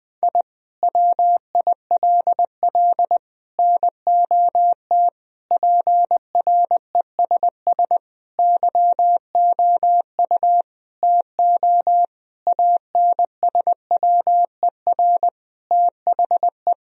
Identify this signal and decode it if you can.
Morse code, heard as a keyed tone, about 20 wpm, 700 Hz. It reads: I WILL NOT PRESS YOU TO ANSWER THE
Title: Sentences from Top 1000 Words - Review Part 1 - 20wpm